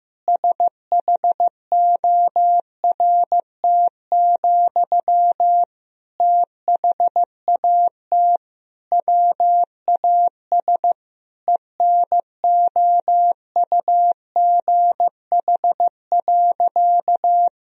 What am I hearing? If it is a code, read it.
Morse code, heard as a keyed tone, about 15 wpm, 700 Hz. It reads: SHORT, THAT WAS ENOUGH.
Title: Sentences from Top 1000 Words - Review Part 2 - 15wpm